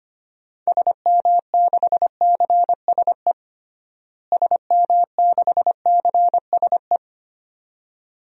Morse code, heard as a keyed tone, about 25 wpm, 700 Hz. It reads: SM6CSE SM6CSE
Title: DXCC-100 Call Signs - 6 Characters - Repeated 2x - 25wpm